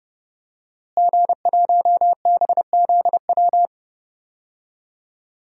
Morse code, heard as a keyed tone, about 30 wpm, 700 Hz. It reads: G1BZW
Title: DXCC-100 Call Signs - 5 Characters - Repeated 3x - 30wpm